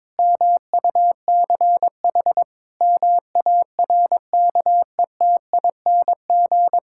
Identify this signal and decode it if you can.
Morse code, heard as a keyed tone, about 22 wpm, 700 Hz. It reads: MUCH MARKETING
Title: Sentences from Top 1000 Words - Review Part 5 - 22wpm